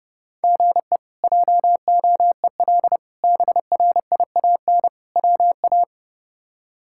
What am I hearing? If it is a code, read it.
Morse code, heard as a keyed tone, about 30 wpm, 700 Hz. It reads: GE JOEL BRIAN WA
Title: K1USN SST QSO Practice - 30wpm